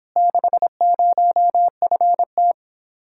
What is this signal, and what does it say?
Morse code, heard as a keyed tone, about 26 wpm, 700 Hz. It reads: 60FT
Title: Speed-Racing Common QSO Phrases Encore - 22wpm